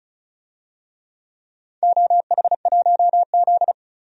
Morse code, heard as a keyed tone, about 36 wpm, 700 Hz. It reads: OH1Z
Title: DXCC-100 Call Signs - 4 Characters - 35wpm